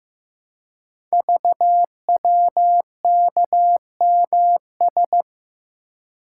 Morse code, heard as a keyed tone, about 15 wpm, 700 Hz. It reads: VWKMS
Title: Code Groups - 5 Letters - 15wpm